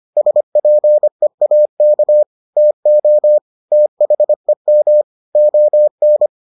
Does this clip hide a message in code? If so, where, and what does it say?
Yes: Morse code from 0.2 to 6.3 s, heard as a keyed tone, about 25 wpm, 600 Hz. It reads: SPEAK TO THEM ON